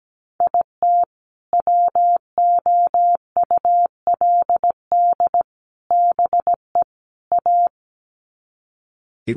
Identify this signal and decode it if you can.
Morse code, heard as a keyed tone, about 17 wpm, 700 Hz. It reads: IT WOULD BE A
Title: Sets of 4 Words 17wpm